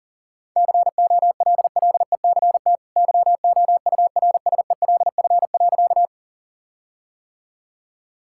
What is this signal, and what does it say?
Morse code, heard as a keyed tone, about 40 wpm, 700 Hz. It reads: COLLECT YOURSELF.
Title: Sentences from Top 1600 Words 40wpm